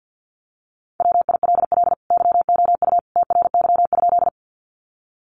Morse code, heard as a keyed tone, about 48 wpm, 700 Hz. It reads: WILL YOU TRY?
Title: Sentences from Top 200 Words 50wpm